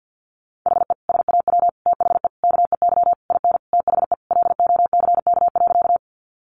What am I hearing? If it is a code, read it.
Morse code, heard as a keyed tone, about 50 wpm, 700 Hz. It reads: HE SAW THE KEY IN THE LOCK.
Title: Sentences from Top 1300 Words 50wpm